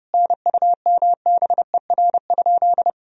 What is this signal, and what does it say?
Morse code, heard as a keyed tone, about 30 wpm, 700 Hz. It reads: NUMBER?
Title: Sentences from Top 1000 Words - Review Part 1 - 30wpm